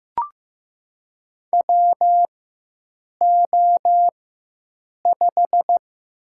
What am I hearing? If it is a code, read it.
Morse code, heard as a keyed tone, about 15 wpm, 700 Hz. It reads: W O 5
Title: USA Call Signs - 4 Characters - 4x Character Spacing - 15wpm